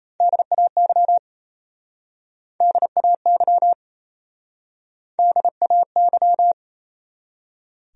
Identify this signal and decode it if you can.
Morse code, heard as a keyed tone, about 32 wpm, 700 Hz. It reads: DAY DAY DAY
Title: Speed-Racing Top 100 Words 28wpm